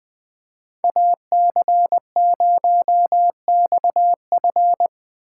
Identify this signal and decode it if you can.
Morse code, heard as a keyed tone, about 20 wpm, 700 Hz. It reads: AC0XF
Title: Speed Racing - USA 2x2 Call Signs 17wpm